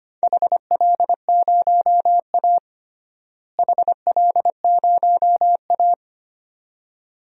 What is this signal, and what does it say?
Morse code, heard as a keyed tone, about 25 wpm, 700 Hz. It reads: HL0A HL0A
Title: DXCC-100 Call Signs - 4 Characters - Repeated 2x - 25wpm